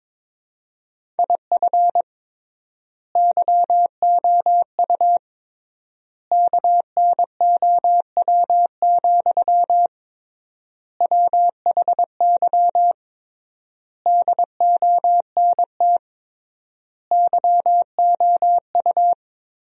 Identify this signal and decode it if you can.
Morse code, heard as a keyed tone, about 22 wpm, 700 Hz. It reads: IF YOU KNOW, WHY DONT YOU
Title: 3x Word-Spacing - Sentences from Top 300 Words - 22wpm